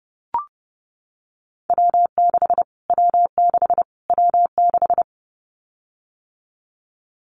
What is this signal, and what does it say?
Morse code, heard as a keyed tone, about 30 wpm, 700 Hz. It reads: W6 W6 W6
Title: Callsign Prefixes for Top 25 DXCC - Repeated 3x - 30wpm